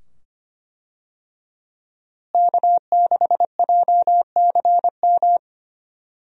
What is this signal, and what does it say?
Morse code, heard as a keyed tone, about 25 wpm, 700 Hz. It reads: K6JCM
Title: USA Call Signs - 5 Characters - Repeated 2x - 25wpm